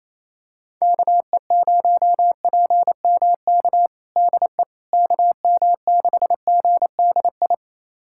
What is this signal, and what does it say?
Morse code, heard as a keyed tone, about 28 wpm, 700 Hz. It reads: KE0PMK DE KM6GDI
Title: USA Callsign DE Callsign - 6 Characters - 28wpm